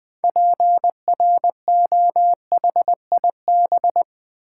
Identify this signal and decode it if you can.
Morse code, heard as a keyed tone, about 20 wpm, 700 Hz. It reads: PROHIB
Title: Build A Word - 2 Letters At A Time - 8-Letter Words - 20wpm